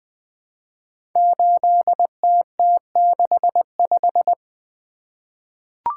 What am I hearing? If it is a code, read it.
Morse code, heard as a keyed tone, about 20 wpm, 700 Hz. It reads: 8TT65
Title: Code Groups - 5 Letter:Numbers - 20wpm